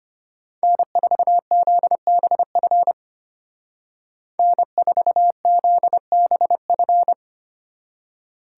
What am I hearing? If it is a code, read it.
Morse code, heard as a keyed tone, about 33 wpm, 700 Hz. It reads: N4ZBF N4ZBF